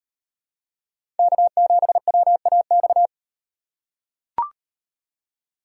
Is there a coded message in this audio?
Yes: Morse code from 1.2 to 3.1 s, heard as a keyed tone, about 38 wpm, 700 Hz. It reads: K7WAX